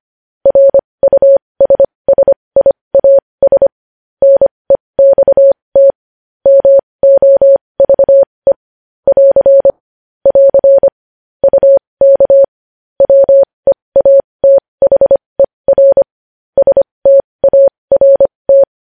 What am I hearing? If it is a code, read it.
Morse code, heard as a keyed tone, about 25 wpm, spaced as for 20 wpm, 550 Hz. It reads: RUSSIAS NEXT MOVE <AR> <AR> UK WEATHER START
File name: podcast / News headlines-1x-25x20